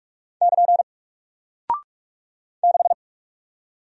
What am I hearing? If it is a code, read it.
Morse code, heard as a keyed tone, about 44 wpm, 700 Hz. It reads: <KN> 6
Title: Single Character (Letter:Number:Punctuation:Prosign) Alt 45wpm